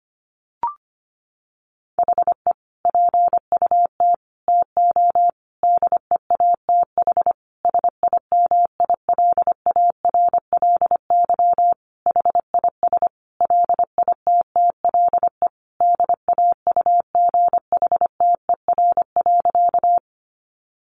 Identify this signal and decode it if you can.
Morse code, heard as a keyed tone, about 25 wpm, 700 Hz. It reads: HE PUT TO DEATH SIMILARLY HIS LITTLE DAUGHTER.
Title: Sentences from Top 1500 Words 25wpm